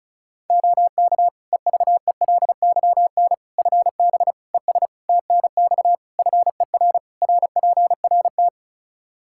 Morse code, heard as a keyed tone, about 36 wpm, 700 Hz. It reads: OK EVELYN FB ES TNX FER RPRT
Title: Practice QSO - Exchange 2 - Info - 35wpm